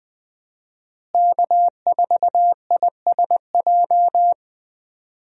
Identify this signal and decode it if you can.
Morse code, heard as a keyed tone, about 20 wpm, 700 Hz. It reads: K4ISJ
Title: USA Call Signs - 5 Characters - Repeated 2x - 20wpm